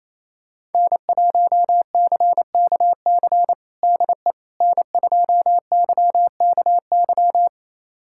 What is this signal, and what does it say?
Morse code, heard as a keyed tone, about 28 wpm, 700 Hz. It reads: N1CKC DE N2YKY
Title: USA Callsign DE Callsign - 5 Characters - 28wpm